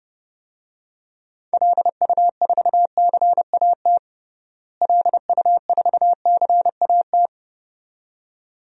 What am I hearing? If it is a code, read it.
Morse code, heard as a keyed tone, about 30 wpm, 700 Hz. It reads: LU4CAT LU4CAT
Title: DXCC-100 Call Signs - 6 Characters - Repeated 2x - 30wpm